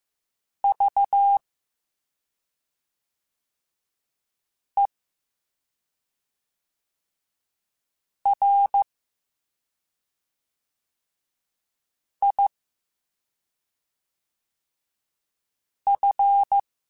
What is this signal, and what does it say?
Morse code, heard as a keyed tone, about 15 wpm, spaced as for 2 wpm, 800 Hz. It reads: VERIF